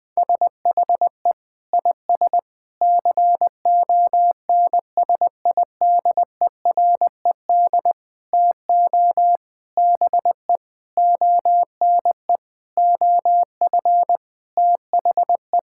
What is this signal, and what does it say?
Morse code, heard as a keyed tone, about 20 wpm, 700 Hz. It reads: SHE IS CONSIDERED TO BE ONE OF THE
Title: Sentences with 8-Word Phrases - 20wpm